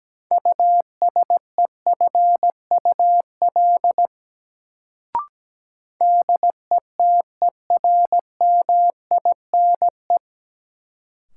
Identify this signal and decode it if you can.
Morse code, heard as a keyed tone, about 17 wpm, 700 Hz. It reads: USEFUL DETERMINE
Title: Sentences from Top 700 Words - New Word Drill - 17wpm